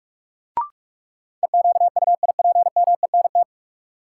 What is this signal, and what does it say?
Morse code, heard as a keyed tone, about 45 wpm, 700 Hz. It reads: EQUIPMENT
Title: Sentences from Top 700 Words - New Word Drill - 45wpm